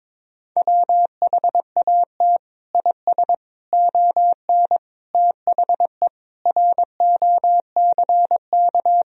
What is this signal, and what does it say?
Morse code, heard as a keyed tone, about 22 wpm, 700 Hz. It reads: WHAT IS ON THE ROCK
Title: Sentences from Top 1000 Words - Review Part 2 - 22wpm